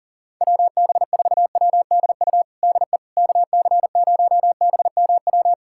Morse code, heard as a keyed tone, about 40 wpm, 700 Hz. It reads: WB4WDU DE KC0BMW
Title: USA Callsign DE Callsign - 6 Characters - 40wpm